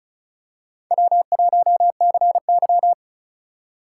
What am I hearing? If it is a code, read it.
Morse code, heard as a keyed tone, about 36 wpm, 690 Hz. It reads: W1CY
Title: USA Call Signs - 4 Characters - 2x Character Spacing - 35wpm